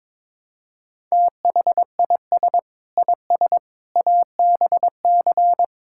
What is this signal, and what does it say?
Morse code, heard as a keyed tone, about 22 wpm, 700 Hz. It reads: THIS IS ABC
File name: practice / 22wpm / Build A Phrase - 5-Word Phrases - 22wpm